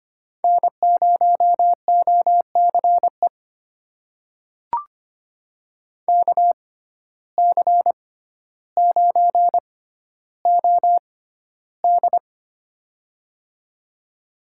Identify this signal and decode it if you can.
Morse code, heard as a keyed tone, about 25 wpm, spaced as for 13 wpm, 700 Hz. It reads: N0OCE KC9OD
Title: USA Call Signs - 5 Characters - 6x Character Spacing - 25wpm